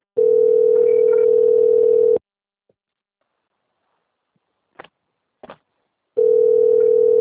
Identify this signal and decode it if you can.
Ringback tone (North America), 440 + 480 Hz.
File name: out- - - - - .18909